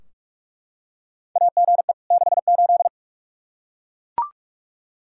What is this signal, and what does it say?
Morse code, heard as a keyed tone, about 42 wpm, 700 Hz. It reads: AGE 68
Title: Practice QSO - Exchange 3 - More Info - 45wpm